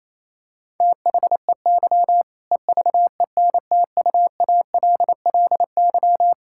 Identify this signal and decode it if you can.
Morse code, heard as a keyed tone, about 28 wpm, 700 Hz. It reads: THEY EVENTUALLY
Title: Sentences from Top 1000 Words - Review Part 6 - 28wpm